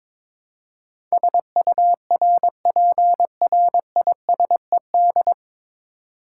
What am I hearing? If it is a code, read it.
Morse code, heard as a keyed tone, about 22 wpm, 700 Hz. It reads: SURPRISED
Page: Morse Code Ninja practice 22wpm